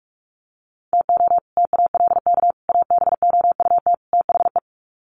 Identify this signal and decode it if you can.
Morse code, heard as a keyed tone, about 44 wpm, 700 Hz. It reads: TO TALK ABOUT THE
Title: Sets of 4 Words 45wpm